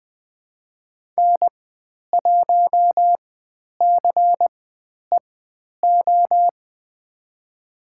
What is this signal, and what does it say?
Morse code, heard as a keyed tone, about 20 wpm, spaced as for 10 wpm, 700 Hz. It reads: N1CEO
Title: 102 Lesson 17g - Letter u - Call Signs 20wpm